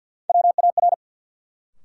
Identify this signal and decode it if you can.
Morse code, heard as a keyed tone, about 53 wpm, 700 Hz. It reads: WAR
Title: Sets of 5 Words - New Word Drill - 50wpm